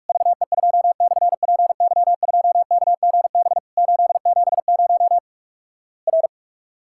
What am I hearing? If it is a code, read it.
Morse code, heard as a keyed tone, about 46 wpm, 700 Hz. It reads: VE2/PY2KGB 870 R
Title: Worked All Europe DX Contest - QTC Exchange - 45wpm